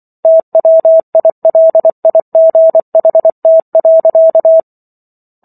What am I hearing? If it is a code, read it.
Morse code, heard as a keyed tone, about 24 wpm, 650 Hz. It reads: TWILIGHT.